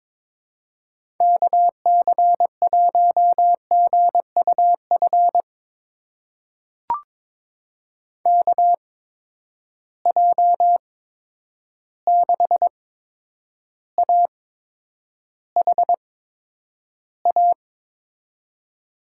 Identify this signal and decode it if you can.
Morse code, heard as a keyed tone, about 22 wpm, spaced as for 9 wpm, 700 Hz. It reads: KC1GUF KJ6AHA
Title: USA Call Signs - 6 Characters - 8x Character Spacing - 22wpm